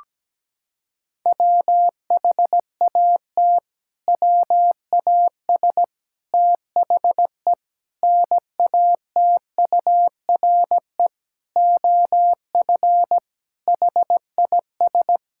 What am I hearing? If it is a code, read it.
Morse code, heard as a keyed tone, about 17 wpm, 700 Hz. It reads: WHAT WAS THE NATURE OF HIS
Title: Sentences from Top 1600 Words 17wpm